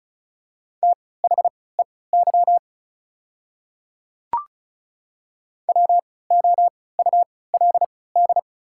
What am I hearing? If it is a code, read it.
Morse code, heard as a keyed tone, about 36 wpm, spaced as for 20 wpm, 690 Hz. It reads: THEY WOULD
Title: 3x Character Spacing - Top 200 Words - 35wpm